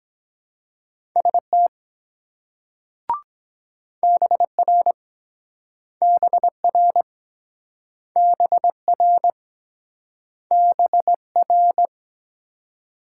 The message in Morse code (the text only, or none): ST BR BR BR BR